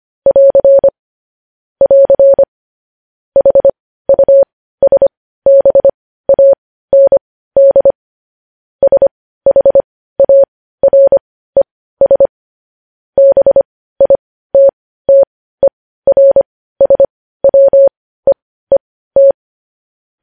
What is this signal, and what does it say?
Morse code, heard as a keyed tone, about 25 wpm, spaced as for 15 wpm, 550 Hz. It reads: <AR> <AR> HUSBAND SHARES BITTERSWEET
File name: podcast / News headlines-1x-25x15